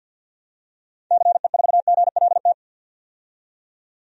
Morse code, heard as a keyed tone, about 50 wpm, 700 Hz. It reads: CE4GLT
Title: DXCC-100 Call Signs - 6 Characters - 50wpm